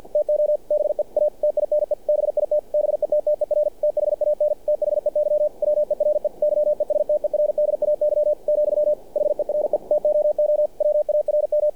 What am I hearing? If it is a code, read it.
Morse code, heard as a keyed tone, about 48 wpm, 600 Hz. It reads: TO BE A TINE BIT BETTER THAN THEY WERE YESTERDAY , HERE TOO WARM
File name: archived / CRSnet